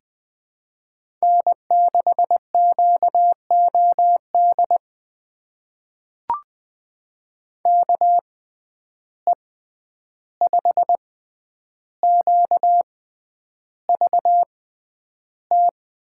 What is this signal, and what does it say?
Morse code, heard as a keyed tone, about 20 wpm, spaced as for 10 wpm, 700 Hz. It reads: N6QOD KE5QVT
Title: USA Call Signs - 6x Character Spacing - 20wpm